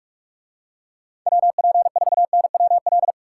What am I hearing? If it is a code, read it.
Morse code, heard as a keyed tone, about 43 wpm, 700 Hz. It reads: WP4NWL